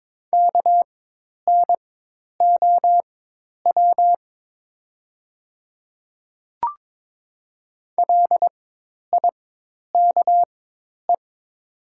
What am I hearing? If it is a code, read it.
Morse code, heard as a keyed tone, about 22 wpm, spaced as for 10 wpm, 700 Hz. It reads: KNOW LIKE
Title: 4x Character Spacing - Top 200 Words - 22wpm